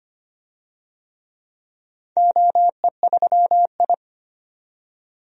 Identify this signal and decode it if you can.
Morse code, heard as a keyed tone, about 25 wpm, 700 Hz. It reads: OE3I
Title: DXCC-100 Call Signs - 4 Characters - 25wpm